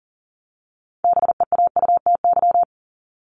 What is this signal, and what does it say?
Morse code, heard as a keyed tone, about 40 wpm, 700 Hz. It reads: BEAUTY
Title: Top 500 Words 40wpm